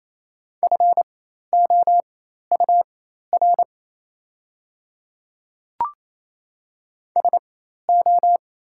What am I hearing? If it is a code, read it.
Morse code, heard as a keyed tone, about 28 wpm, spaced as for 13 wpm, 700 Hz. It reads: FOUR SO
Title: 4x Character Spacing - Top 500 Words - 28wpm